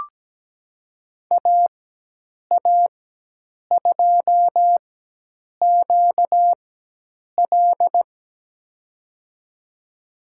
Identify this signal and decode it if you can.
Morse code, heard as a keyed tone, about 17 wpm, spaced as for 8 wpm, 700 Hz. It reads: AA2QL
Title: USA Call Signs - 5 Characters - 4x Character Spacing - 17wpm